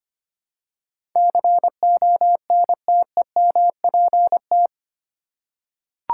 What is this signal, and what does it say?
Morse code, heard as a keyed tone, about 25 wpm, 700 Hz. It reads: CONTEMPT
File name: 8 Letter Words - First 3 Letters Given - 25wpm